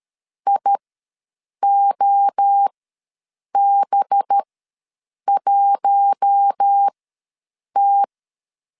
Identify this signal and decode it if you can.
Morse code, heard as a keyed tone, about 13 wpm, spaced as for 7 wpm, 800 Hz. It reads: IOB1T
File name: LEZIONI / 8 YJ45 / 8 YJ45 ws 12wpm 3146